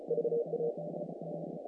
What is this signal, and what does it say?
Morse code, heard as a keyed tone, about 32 wpm, 460 Hz, with noise. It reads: VA
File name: contests / naqp-cw-k5tr-op-k5pi- -2100